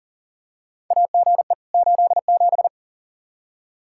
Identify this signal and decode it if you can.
Morse code, heard as a keyed tone, about 40 wpm, 700 Hz. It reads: AGE 87
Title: K1USN SST QSO Practice 40wpm